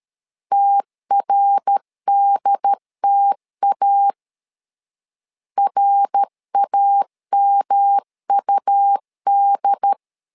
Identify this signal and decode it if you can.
Morse code, heard as a keyed tone, about 13 wpm, 800 Hz. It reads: TRDTA RAMUD
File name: LEZIONI / 3 MWRO / 3 MWR 12wpm 1711